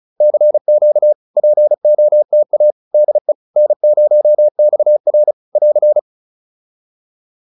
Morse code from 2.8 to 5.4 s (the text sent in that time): DE N0XR